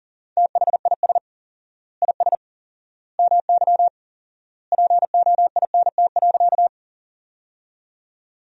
Morse code, heard as a keyed tone, about 40 wpm, 700 Hz. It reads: THIS IS MY POINT.
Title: 4x Word-Spacing - Sentences from Top 200 Words - 40wpm